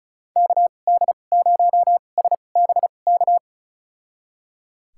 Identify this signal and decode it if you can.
Morse code, heard as a keyed tone, about 34 wpm, spaced as for 25 wpm, 700 Hz. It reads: KD0SBK
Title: USA Call Signs - 6 Characters - 2x Character Spacing - 35wpm